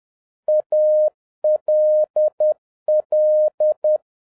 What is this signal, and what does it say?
Morse code, heard as a keyed tone, about 10 wpm, 610 Hz. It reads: ALL
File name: 2011-07-mp3 / n21 GREETINGS FROM SINGAPORE 10wpm